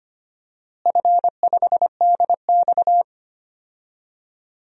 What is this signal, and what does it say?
Morse code, heard as a keyed tone, about 25 wpm, 700 Hz. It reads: F5DX
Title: DXCC-100 Call Signs - 4 Characters - Repeated 3x - 25wpm